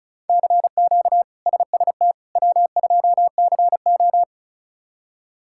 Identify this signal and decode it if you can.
Morse code, heard as a keyed tone, about 36 wpm, 700 Hz. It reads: CQ SST W2CO
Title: K1USN SST QSO Practice - 35wpm